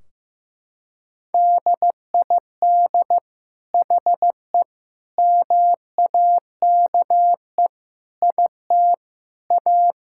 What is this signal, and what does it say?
Morse code, heard as a keyed tone, about 15 wpm, 700 Hz. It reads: DID HE MAKE IT A
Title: Sentences from Top 1600 Words 15wpm